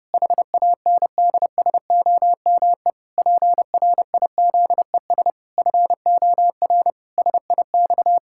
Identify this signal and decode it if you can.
Morse code, heard as a keyed tone, about 32 wpm, 700 Hz. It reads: HANDSOME PRIZES FOR SIX
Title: Pangram Challenge v2 - 30wpm